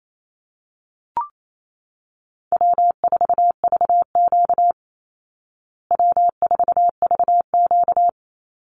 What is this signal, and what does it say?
Morse code, heard as a keyed tone, about 28 wpm, 700 Hz. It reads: W4VQ W4VQ
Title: Call Signs Repeated 2x - 28wpm